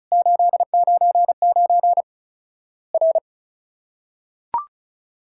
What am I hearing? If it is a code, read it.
Morse code, heard as a keyed tone, about 36 wpm, 700 Hz. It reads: 899 R
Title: Worked All Europe DX Contest - QTC Exchange - 35wpm